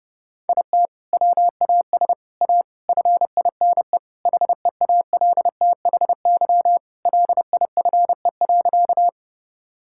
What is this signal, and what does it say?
Morse code, heard as a keyed tone, about 30 wpm, 700 Hz. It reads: IT WAS A FINE HEALTHY LIFE.
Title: Sentences from Top 1000 Words - Review Part 5 - 30wpm